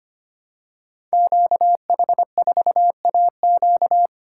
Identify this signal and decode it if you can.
Morse code, heard as a keyed tone, about 25 wpm, 700 Hz. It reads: QH4AQ
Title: Code Groups - 5 Letter:Numbers - 25wpm